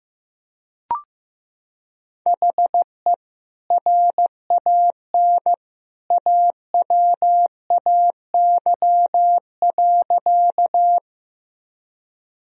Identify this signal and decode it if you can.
Morse code, heard as a keyed tone, about 15 wpm, 700 Hz. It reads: HE RAN AWAY.
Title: Sentences from Top 1000 Words - Review Part 3 - 15wpm